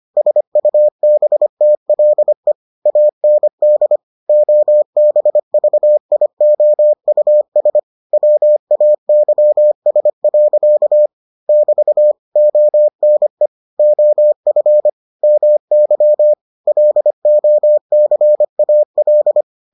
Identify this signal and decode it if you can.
Morse code, heard as a keyed tone, about 25 wpm, 600 Hz. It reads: SUBTLE AND OBVIOUS WAYS. <BT> ONE OF MY LOCAL